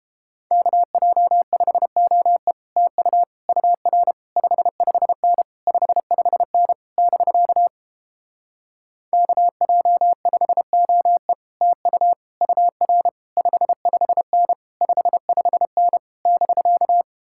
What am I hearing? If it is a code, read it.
Morse code, heard as a keyed tone, about 31 wpm, 700 Hz. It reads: KJ5OE TU UR 55N 55N <BK> KJ5OE TU UR 55N 55N <BK>